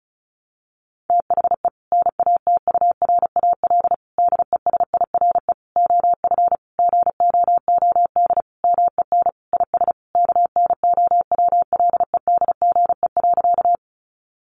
Morse code, heard as a keyed tone, about 34 wpm, 700 Hz. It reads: THE NATURAL DESIRE OF GOOD MEN IS KNOWLEDGE.
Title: Sentences from Top 1100 Words 35wpm